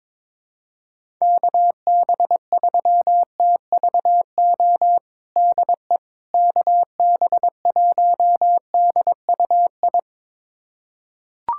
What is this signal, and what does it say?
Morse code, heard as a keyed tone, about 22 wpm, 700 Hz. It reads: KB3TVO DE KB1DUI
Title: USA Callsign DE Callsign - 6 Characters - 22wpm